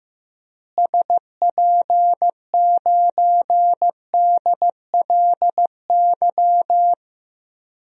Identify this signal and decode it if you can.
Morse code, heard as a keyed tone, about 15 wpm, 700 Hz. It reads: SP9DLY
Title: DXCC-100 Call Signs - 6 Characters - Repeated 3x - 15wpm